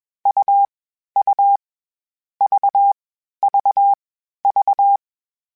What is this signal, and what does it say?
Morse code, heard as a keyed tone, about 21 wpm, 800 Hz. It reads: U U V V V
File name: CW Code Groups-mp3s / Sets mp3s / AUV AUV4 sp @20